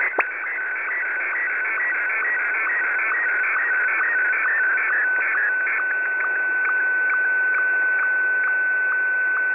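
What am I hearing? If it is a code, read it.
SSTV (slow-scan television), Martin 1.
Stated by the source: Received on a WebSDR at 8190.0 kHz.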